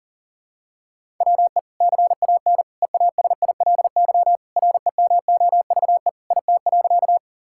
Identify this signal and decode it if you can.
Morse code, heard as a keyed tone, about 40 wpm, 700 Hz. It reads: WE CAN EASILY REMOVE IT.
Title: Sentences from Top 1000 Words - Review Part 5 - 40wpm